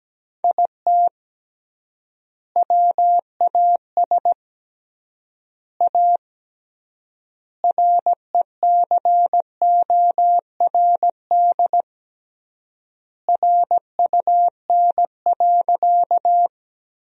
Morse code, heard as a keyed tone, about 17 wpm, 700 Hz. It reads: IT WAS A RECORD RUN.
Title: 3x Word-Spacing - Sentences from Top 500 Words - 17wpm